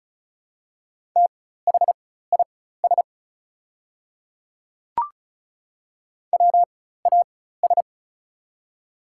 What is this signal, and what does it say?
Morse code, heard as a keyed tone, about 36 wpm, spaced as for 16 wpm, 700 Hz. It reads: THIS WAS
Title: 4x Character Spacing - Top 500 Words - 35wpm